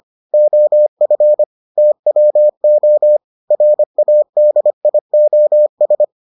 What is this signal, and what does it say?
Morse code, heard as a keyed tone, about 25 wpm, 600 Hz. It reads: OF TWO RADIOS